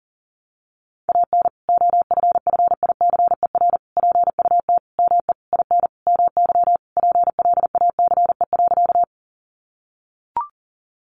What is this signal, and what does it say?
Morse code, heard as a keyed tone, about 40 wpm, 700 Hz. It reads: AN OFFICER PUT ME IN MY PLACE.
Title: Sentences from Top 1200 Words 40wpm